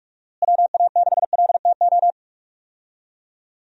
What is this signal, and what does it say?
Morse code, heard as a keyed tone, about 46 wpm, 700 Hz. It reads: WA6LTO
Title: USA Call Signs - 6 Characters - Repeated 2x - 45wpm